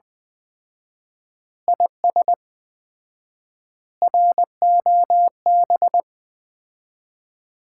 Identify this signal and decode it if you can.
Morse code, heard as a keyed tone, about 21 wpm, 700 Hz. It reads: IS ROB
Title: Common QSO Phrases - 4x Word-Spacing - 20wpm